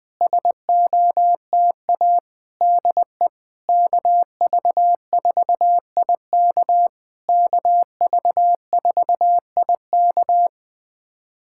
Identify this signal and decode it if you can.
Morse code, heard as a keyed tone, about 20 wpm, 700 Hz. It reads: SOTA DE KV4IK KV4IK
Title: SOTA QSO Practice 20wpm